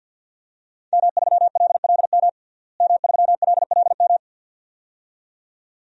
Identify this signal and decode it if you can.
Morse code, heard as a keyed tone, about 49 wpm, 700 Hz. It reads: M3LLM M3LLM
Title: DXCC-100 Call Signs - 5 Characters - Repeated 2x - 50wpm